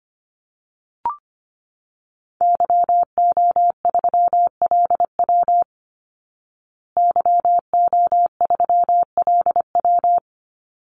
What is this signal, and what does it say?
Morse code, heard as a keyed tone, about 25 wpm, 700 Hz. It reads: YO3LW YO3LW
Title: Call Signs Repeated 2x - 25wpm